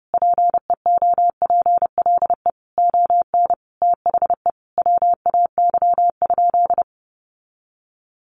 Morse code, heard as a keyed tone, about 30 wpm, 700 Hz. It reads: PEOPLE ON THE WAY?
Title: Sentences from Top 1100 Words 30wpm